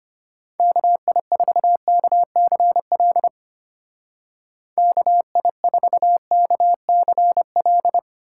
Morse code, heard as a keyed tone, about 32 wpm, 700 Hz. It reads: KI4KCL KI4KCL